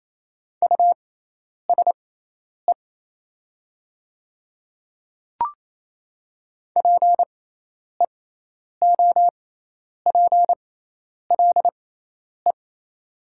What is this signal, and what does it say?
Morse code, heard as a keyed tone, about 28 wpm, spaced as for 10 wpm, 700 Hz. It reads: USE PEOPLE
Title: 6x Character Spacing - Top 300 Words - 28wpm